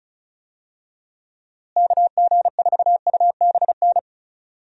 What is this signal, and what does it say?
Morse code, heard as a keyed tone, about 36 wpm, 700 Hz. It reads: KG4UBN
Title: USA Call Signs - 6 Characters - Repeated 3x - 35wpm